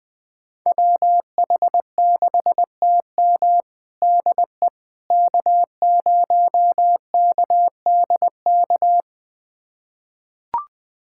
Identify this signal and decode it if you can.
Morse code, heard as a keyed tone, about 21 wpm, 700 Hz. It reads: WH6TM DE K0KDK
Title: USA Callsign DE Callsign - 5 Characters - 20wpm